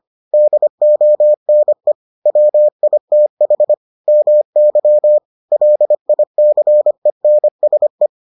Morse code, heard as a keyed tone, about 25 wpm, 600 Hz. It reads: DONE WITH MY LICENSE